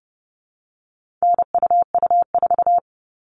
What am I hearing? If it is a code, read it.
Morse code, heard as a keyed tone, about 30 wpm, 700 Hz. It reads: NUU4
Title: All Cut Numbers - 4-Digit 30wpm